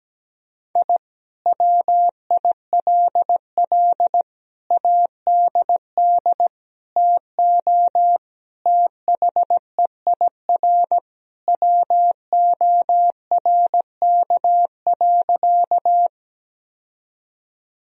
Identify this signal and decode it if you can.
Morse code, heard as a keyed tone, about 17 wpm, 700 Hz. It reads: I WILL ADD TO THEIR WORK.
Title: Sentences from Top 1000 Words - Review Part 1 - 17wpm